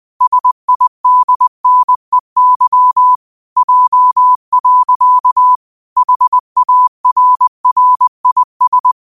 Morse code, heard as a keyed tone, about 20 wpm, 1000 Hz. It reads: SIDNEY J. HARRIS